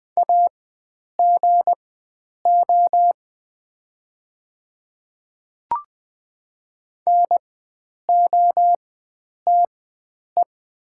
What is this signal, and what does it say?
Morse code, heard as a keyed tone, about 20 wpm, spaced as for 9 wpm, 700 Hz. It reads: AGO NOTE
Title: 4x Character Spacing - Top 500 Words - 20wpm